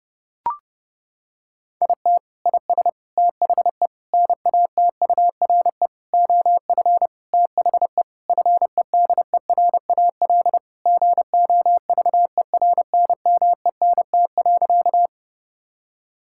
Morse code, heard as a keyed tone, about 30 wpm, 700 Hz. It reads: IT IS THE NATURE OF THE FEDERAL GOVERNMENT.
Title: Sentences from Top 1000 Words - Review Part 5 - 30wpm